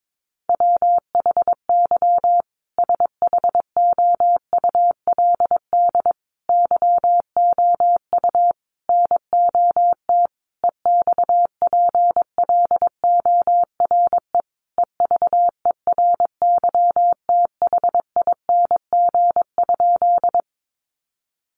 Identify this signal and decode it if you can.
Morse code, heard as a keyed tone, about 22 wpm, 700 Hz. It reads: WHY SHOULD YOU NOT EXPLORE EVERYTHING?
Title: Sentences from Top 1100 Words 22wpm